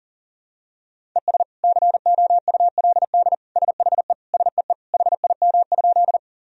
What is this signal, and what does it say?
Morse code, heard as a keyed tone, about 40 wpm, 700 Hz. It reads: ES COULD SHE SEE HIM?
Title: Sentences from Top 100 Words and Abbreviations 40wpm